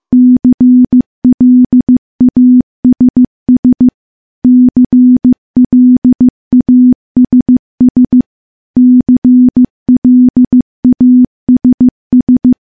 Morse code, heard as a keyed tone, about 15 wpm, 250 Hz. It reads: CLASS CLASS CLASS
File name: morse / morse-training-15WPM